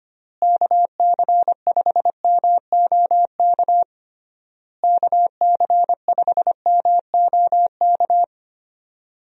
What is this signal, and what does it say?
Morse code, heard as a keyed tone, about 25 wpm, 700 Hz. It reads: KC5MOK KC5MOK
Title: USA Call Signs - 6 Characters - Repeated 3x - 25wpm